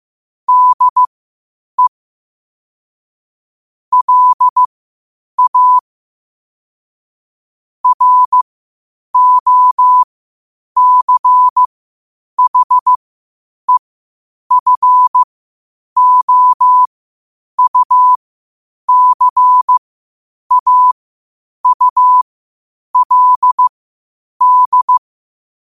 Morse code, heard as a keyed tone, about 15 wpm, spaced as for 8 wpm, 1000 Hz. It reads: DE LA ROCHEFOUCAULD
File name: qotdfiles / 8wpm